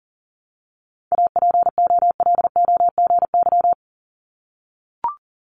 Morse code, heard as a keyed tone, about 38 wpm, 700 Hz. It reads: APOLOGY
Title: Multi-letter Phonemes 40wpm